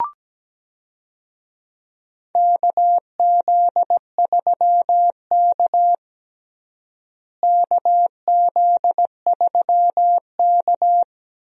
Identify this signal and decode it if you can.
Morse code, heard as a keyed tone, about 17 wpm, 700 Hz. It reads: KZ3K KZ3K
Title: USA Call Signs - 4 Characters - Repeated 3x - 17wpm